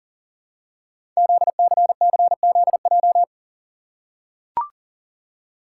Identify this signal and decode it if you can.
Morse code, heard as a keyed tone, about 42 wpm, 700 Hz. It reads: ZCCZJ
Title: Code Groups - 5 Letters - 40wpm